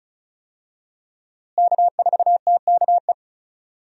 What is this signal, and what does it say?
Morse code, heard as a keyed tone, about 36 wpm, 700 Hz. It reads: K4TKE
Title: USA Call Signs - 5 Characters - Repeated 3x - 35wpm